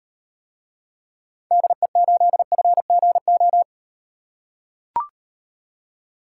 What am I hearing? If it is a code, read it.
Morse code, heard as a keyed tone, about 38 wpm, 700 Hz. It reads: DE8FGO